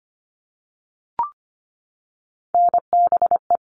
Morse code, heard as a keyed tone, about 25 wpm, 700 Hz. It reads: NBE
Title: All Cut Numbers - 3-Digit 25wpm